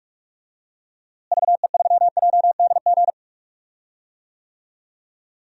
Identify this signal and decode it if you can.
Morse code, heard as a keyed tone, about 43 wpm, 700 Hz. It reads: VE3JDG
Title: DXCC-100 Call Signs - 6 Characters - Repeated 3x - 45wpm